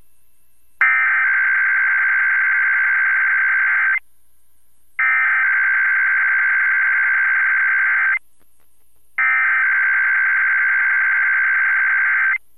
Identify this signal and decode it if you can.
Emergency Alert System (SAME) tones (header bursts), heard as harsh buzzing bursts. It is heard from 0.8 to 12.3 s.